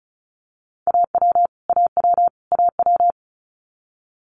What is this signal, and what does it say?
Morse code, heard as a keyed tone, about 36 wpm, 700 Hz. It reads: AW AW AW
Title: Multi-letter Phonemes Repeated 3x 35wpm